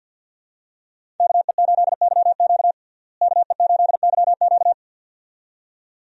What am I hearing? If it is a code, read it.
Morse code, heard as a keyed tone, about 47 wpm, 700 Hz. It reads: KE8YQ KE8YQ
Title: USA Call Signs - 5 Characters - Repeated 2x - 50wpm